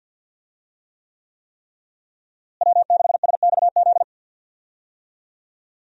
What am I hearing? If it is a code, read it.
Morse code, heard as a keyed tone, about 46 wpm, 700 Hz. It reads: W6SXZ
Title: USA Call Signs - 5 Characters - Repeated 3x - 50wpm